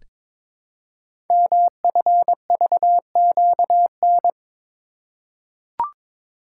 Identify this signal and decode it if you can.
Morse code, heard as a keyed tone, about 22 wpm, 700 Hz. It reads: MFVQN
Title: Code Groups - 5 Letters - 22wpm